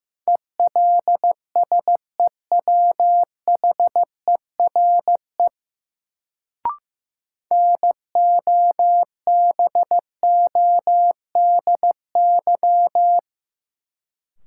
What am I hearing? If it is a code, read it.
Morse code, heard as a keyed tone, about 15 wpm, 700 Hz. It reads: ELSEWHERE NOBODY
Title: Sentences from Top 1600 Words - New Word Drill Alt - 15wpm